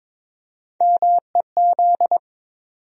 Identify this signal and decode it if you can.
Morse code, heard as a keyed tone, about 22 wpm, 700 Hz. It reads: MEZ
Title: USA Call Sign Postfixes - 3 Letters 22wpm